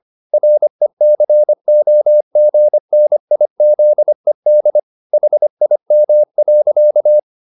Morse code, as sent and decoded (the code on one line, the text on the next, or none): .-. -.-. --- --. -. .. --.. -.. / .... .. -- .-.-.-
RECOGNIZED HIM.